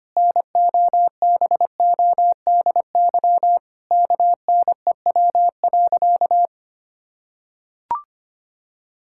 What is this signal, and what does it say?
Morse code, heard as a keyed tone, about 25 wpm, 700 Hz. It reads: NOBODY KNEW.
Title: Sentences from Top 1600 Words 25wpm